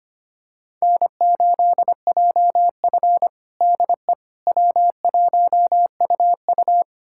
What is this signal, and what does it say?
Morse code, heard as a keyed tone, about 25 wpm, 700 Hz. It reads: N8JF DE W1UU
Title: USA Callsign DE Callsign - 4 Characters - 25wpm